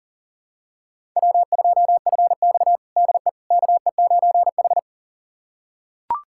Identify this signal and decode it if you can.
Morse code, heard as a keyed tone, about 37 wpm, 700 Hz. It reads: W2FX DE KE9H
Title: USA Callsign DE Callsign - 4 Characters - 40wpm